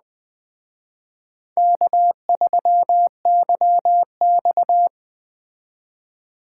Morse code, heard as a keyed tone, about 21 wpm, 700 Hz. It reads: K3YX